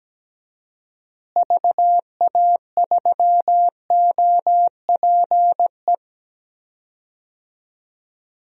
Morse code, heard as a keyed tone, about 17 wpm, 700 Hz. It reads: VA3OPE
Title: DXCC-100 Call Signs - 6 Characters - Repeated 2x - 17wpm